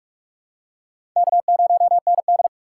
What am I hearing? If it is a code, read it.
Morse code, heard as a keyed tone, about 44 wpm, 700 Hz. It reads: K0ND
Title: USA Call Signs - 4x Character Spacing - 45wpm